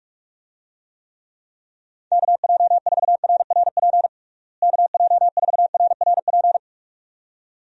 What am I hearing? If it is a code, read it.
Morse code, heard as a keyed tone, about 41 wpm, 700 Hz. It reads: KJ4RRP KJ4RRP
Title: USA Call Signs - 6 Characters - Repeated 2x - 45wpm